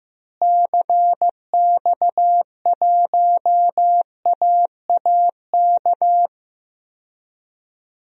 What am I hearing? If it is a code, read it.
Morse code, heard as a keyed tone, about 15 wpm, 700 Hz. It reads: CX1AAK
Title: DXCC-100 Call Signs - 6 Characters - Repeated 2x - 15wpm